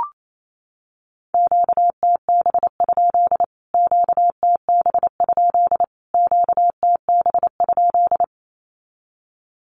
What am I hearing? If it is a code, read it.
Morse code, heard as a keyed tone, about 28 wpm, 700 Hz. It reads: QTB? QTB? QTB?
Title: Q-Codes Repeated 3x 28wpm